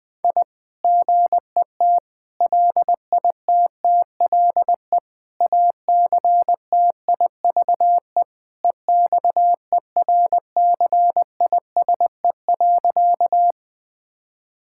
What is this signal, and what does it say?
Morse code, heard as a keyed tone, about 20 wpm, 700 Hz. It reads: I GET LITTLE ACTIVE EXERCISE.
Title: Sentences from Top 1000 Words - Review Part 6 - 20wpm